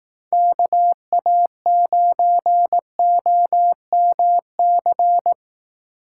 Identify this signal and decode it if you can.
Morse code, heard as a keyed tone, about 18 wpm, 700 Hz. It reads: KA9OMC
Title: Speed Racing - USA Call Signs 15wpm